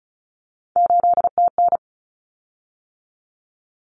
Morse code, heard as a keyed tone, about 34 wpm, 700 Hz. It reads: 8TN
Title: Common Cut Numbers - 3-Digit 35wpm